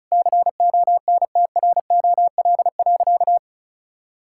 Morse code, heard as a keyed tone, about 36 wpm, 700 Hz. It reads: CONTROL.
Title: Sentences from Top 1000 Words - Review Part 5 - 35wpm